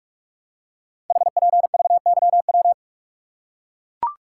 Morse code, heard as a keyed tone, about 44 wpm, 700 Hz. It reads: HPVYW
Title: Code Groups - 5 Letters - 45wpm